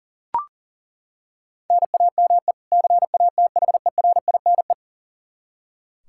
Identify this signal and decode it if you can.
Morse code, heard as a keyed tone, about 40 wpm, 700 Hz. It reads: NAME CATHERINE
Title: Practice QSO - Exchange 1 - Beginning - 40wpm